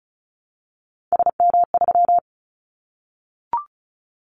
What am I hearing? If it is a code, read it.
Morse code, heard as a keyed tone, about 34 wpm, 700 Hz. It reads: SM3
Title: Callsign Prefixes for Top 25 DXCC - Repeated 3x - 35wpm